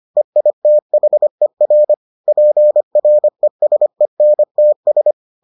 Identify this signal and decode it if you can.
Morse code, heard as a keyed tone, about 25 wpm, 600 Hz. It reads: EITHER PRESENTS